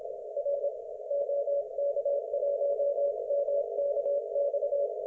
Morse code, heard as a keyed tone, about 25 wpm, 570 Hz. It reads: R OK1MKX
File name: Archiv / CUC216 / ok1fhi cul216